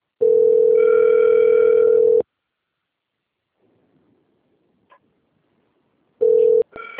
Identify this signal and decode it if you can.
Ringback tone (North America), 440 + 480 Hz.